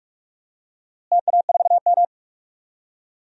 Morse code, heard as a keyed tone, about 45 wpm, 690 Hz. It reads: TA4M
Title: DXCC-100 Call Signs - 4 Characters - Repeated 3x - 45wpm